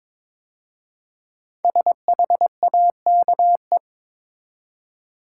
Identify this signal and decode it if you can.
Morse code, heard as a keyed tone, about 22 wpm, 700 Hz. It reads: SHAKE
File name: 5 Letter Words - First 3 Letters Given - 22wpm